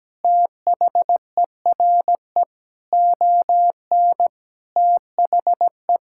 Morse code, heard as a keyed tone, about 17 wpm, 700 Hz. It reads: THERE ON THE
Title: Sentences from Top 1000 Words - Review Part 5 - 17wpm